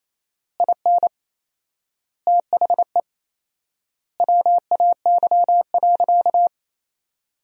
Morse code, heard as a keyed tone, about 28 wpm, 700 Hz. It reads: IN THE WAY.
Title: 4x Word-Spacing - Sentences from Top 400 Words - 28wpm